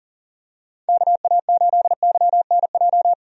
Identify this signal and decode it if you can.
Morse code, heard as a keyed tone, about 40 wpm, 700 Hz. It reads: KA8YNJ
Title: USA Call Signs - 6 Characters - 40wpm